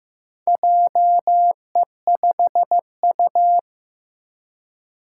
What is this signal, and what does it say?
Morse code, heard as a keyed tone, about 15 wpm, 700 Hz. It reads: JE5U
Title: Four Challenging Letter-Number Combinations 15wpm